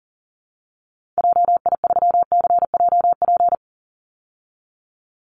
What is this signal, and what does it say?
Morse code, heard as a keyed tone, about 40 wpm, 700 Hz. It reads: JI3CJP
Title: Call Signs Repeated 3x - 40wpm